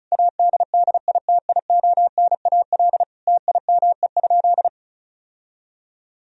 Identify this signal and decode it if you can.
Morse code, heard as a keyed tone, about 36 wpm, 700 Hz. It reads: ADDITIONAL TIME?
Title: Sentences from Top 1000 Words - Review Part 4 - 35wpm